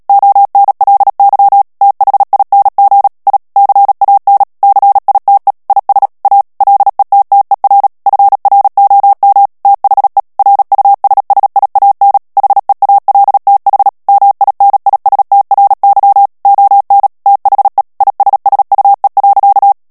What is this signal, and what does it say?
Morse code, heard as a keyed tone, about 37 wpm, 800 Hz. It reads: ONLY THING I CAN CITE IS A LETTER FROM THE RUSSIAN HEALTH MINISTRY ON THE ISSUE.
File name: morse mp3 / ANGLAIS texte clair 2 vitesse 35 mots minute